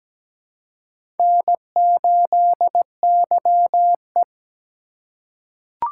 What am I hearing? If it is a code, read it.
Morse code, heard as a keyed tone, about 17 wpm, 700 Hz. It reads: N8YE
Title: USA Call Signs - 4x Character Spacing - 17wpm